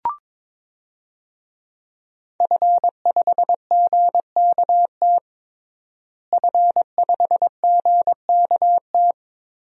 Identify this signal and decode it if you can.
Morse code, heard as a keyed tone, about 21 wpm, 700 Hz. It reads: F5GKT F5GKT
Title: DXCC-100 Call Signs - 5 Characters - Repeated 3x - 22wpm